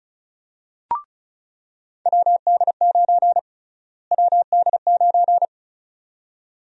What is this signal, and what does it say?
Morse code, heard as a keyed tone, about 36 wpm, 700 Hz. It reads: WD9 WD9
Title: USA Call Sign Prefixes - Repeated 2x - 35wpm